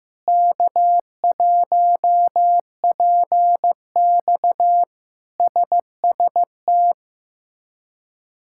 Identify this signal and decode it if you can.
Morse code, heard as a keyed tone, about 15 wpm, 700 Hz. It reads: K1PX SST
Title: K1USN SST QSO Practice - 15wpm